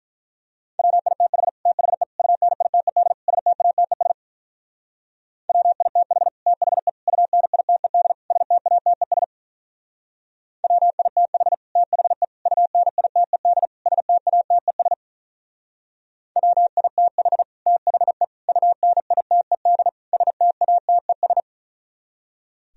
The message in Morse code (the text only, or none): WITH THE UNITED STATES WITH THE UNITED STATES WITH THE UNITED STATES WITH THE UNITED STATES